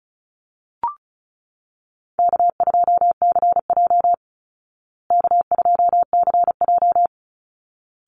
Morse code, heard as a keyed tone, about 34 wpm, 700 Hz. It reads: K2CJ K2CJ
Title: Call Signs Repeated 3x - 35wpm